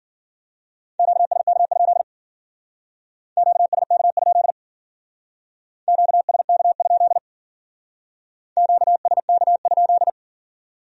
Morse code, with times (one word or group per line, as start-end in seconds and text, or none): 1.0-2.0 QSK?
3.4-4.5 QSK?
5.9-7.2 QSK?
8.6-10.1 QSK?